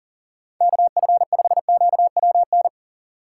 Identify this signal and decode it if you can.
Morse code, heard as a keyed tone, about 42 wpm, 700 Hz. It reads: KF5QWN
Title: USA Call Signs - 6 Characters - 8x Character Spacing - 40wpm